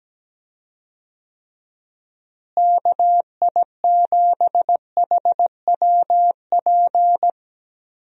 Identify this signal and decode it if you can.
Morse code, heard as a keyed tone, about 17 wpm, 700 Hz. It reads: KI7HWP
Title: USA Call Signs - 6 Characters - Repeated 3x - 17wpm